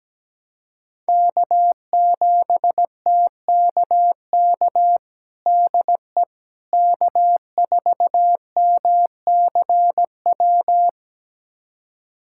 Morse code, heard as a keyed tone, about 17 wpm, 700 Hz. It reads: K7TKK DE K4MCW
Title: USA Callsign DE Callsign - 5 Characters - 17wpm